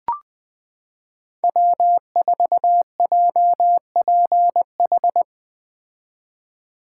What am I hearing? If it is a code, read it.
Morse code, heard as a keyed tone, about 20 wpm, 700 Hz. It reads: W4JPH
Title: K1USN SST QSO Practice - 20wpm